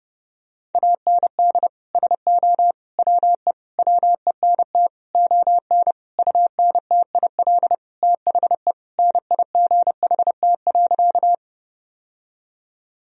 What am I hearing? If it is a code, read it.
Morse code, heard as a keyed tone, about 32 wpm, 700 Hz. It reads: AND SO WE WENT ON UNTIL THE NIGHT.
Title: Sentences from Top 1000 Words - Review Part 1 - 30wpm